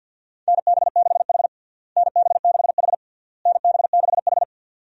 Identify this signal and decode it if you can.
Morse code, heard as a keyed tone, about 52 wpm, 700 Hz. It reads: NB6H NB6H NB6H
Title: USA Call Signs - 4 Characters - Repeated 3x - 50wpm